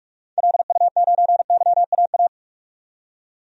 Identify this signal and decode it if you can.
Morse code, heard as a keyed tone, about 46 wpm, 700 Hz. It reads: LU9YAA
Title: DXCC-100 Call Signs - 6 Characters - 45wpm